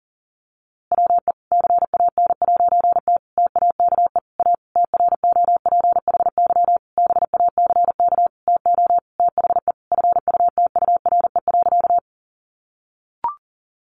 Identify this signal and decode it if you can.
Morse code, heard as a keyed tone, about 40 wpm, 700 Hz. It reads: WE CAN'T TAKE A TROPHY BACK TO THE FUTURE.
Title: A Sound Of Thunder 40wpm